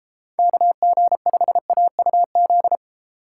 Morse code, heard as a keyed tone, about 32 wpm, 700 Hz. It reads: KG5AUZ